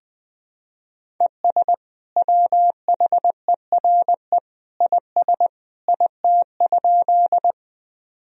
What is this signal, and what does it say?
Morse code, heard as a keyed tone, about 20 wpm, 700 Hz. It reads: ES WHERE IS IT?
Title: Sentences from Top 200 Words and Abbreviations 20wpm